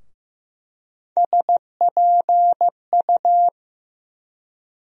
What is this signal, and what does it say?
Morse code, heard as a keyed tone, about 15 wpm, 700 Hz. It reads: SPU
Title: USA Call Sign Postfixes - 3 Letters 15wpm